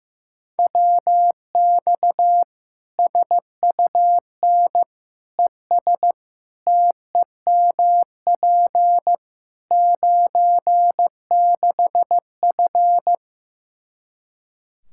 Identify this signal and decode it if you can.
Morse code, heard as a keyed tone, about 15 wpm, 700 Hz. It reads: WX SUN ES TEMP 96F
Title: Practice QSO - Exchange 2 - Info - 15wpm